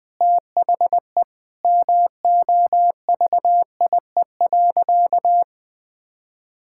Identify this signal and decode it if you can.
Morse code, heard as a keyed tone, about 21 wpm, 700 Hz. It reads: THE MOVIE.
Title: Sentences with 4-Word Phrases - 20wpm